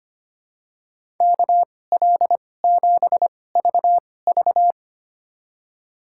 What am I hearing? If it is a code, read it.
Morse code, heard as a keyed tone, about 25 wpm, spaced as for 18 wpm, 700 Hz. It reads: KL7VV
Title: USA Call Signs - 5 Characters - 2x Character Spacing - 25wpm